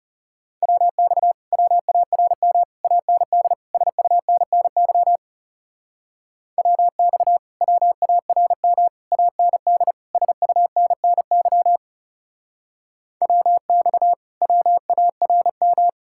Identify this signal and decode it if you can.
Morse code, heard as a keyed tone, about 40 wpm, 700 Hz. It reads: WX WARM AND SUNNY WX WARM AND SUNNY WX WARM